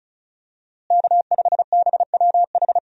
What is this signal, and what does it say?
Morse code, heard as a keyed tone, about 34 wpm, 700 Hz. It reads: K5BWH
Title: USA Call Signs - 5 Characters - Repeated 2x - 35wpm